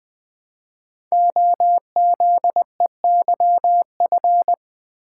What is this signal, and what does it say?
Morse code, heard as a keyed tone, about 20 wpm, 700 Hz. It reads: OZEYF
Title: Code Groups - 5 Letter:Numbers - 20wpm